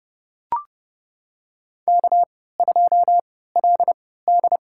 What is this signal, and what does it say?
Morse code, heard as a keyed tone, about 30 wpm, spaced as for 17 wpm, 700 Hz. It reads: K2LD
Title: USA Call Signs - 4 Characters - 3x Character Spacing - 30wpm